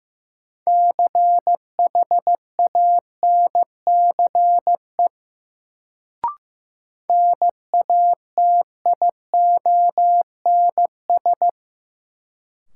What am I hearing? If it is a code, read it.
Morse code, heard as a keyed tone, about 15 wpm, 700 Hz. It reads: CHANCE NATIONS
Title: Sets of 3 Words - New Word Drill - 15wpm